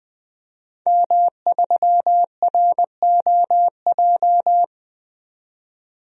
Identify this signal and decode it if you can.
Morse code, heard as a keyed tone, about 21 wpm, 700 Hz. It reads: M3ROJ
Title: DXCC-100 Call Signs - 5 Characters - 20wpm